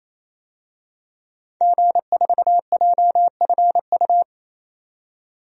Morse code, heard as a keyed tone, about 28 wpm, 700 Hz. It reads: G4JFU
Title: DXCC-100 Call Signs - 5 Characters - 28wpm